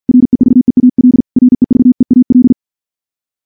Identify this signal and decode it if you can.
Morse code, heard as a keyed tone, about 46 wpm, 250 Hz. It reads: REVEAL REVEAL
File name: morse / morse-training-50WPM